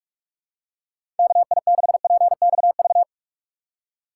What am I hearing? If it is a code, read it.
Morse code, heard as a keyed tone, about 46 wpm, 700 Hz. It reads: KI6PXV
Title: USA Call Signs - 6 Characters - 8x Character Spacing - 45wpm